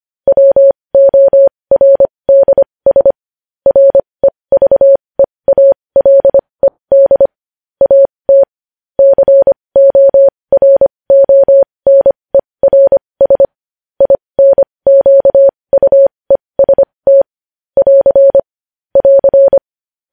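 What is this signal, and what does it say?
Morse code, heard as a keyed tone, about 25 wpm, spaced as for 20 wpm, 550 Hz. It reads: WORDS REVEALED AT CORONERS INQUEST <AR> <AR>
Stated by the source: Morse code news headlines sent at 20 wpm.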